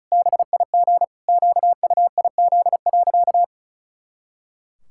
Morse code, heard as a keyed tone, about 36 wpm, 700 Hz. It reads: BIG QUIZ.
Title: Pangram Challenge v2 - 35wpm